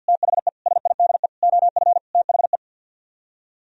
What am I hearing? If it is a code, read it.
Morse code, heard as a keyed tone, about 63 wpm, 700 Hz. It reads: THE SIDE OF THE